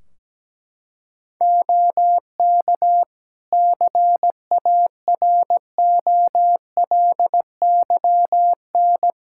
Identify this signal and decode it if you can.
Morse code, heard as a keyed tone, about 17 wpm, 700 Hz. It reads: OK CAROLYN
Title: Practice QSO - Exchange 4 - Ending - 17wpm